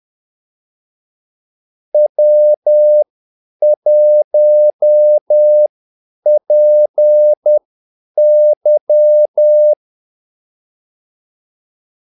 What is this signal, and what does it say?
Morse code, heard as a keyed tone, about 10 wpm, spaced as for 8 wpm, 600 Hz. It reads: W1PY
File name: Callsigns-10-8wpm